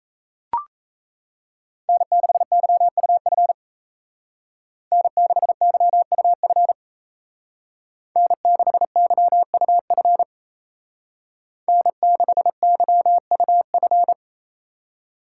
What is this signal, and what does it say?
Morse code, heard as a keyed tone, about 36 wpm, 700 Hz. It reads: N6YUF N6YUF N6YUF N6YUF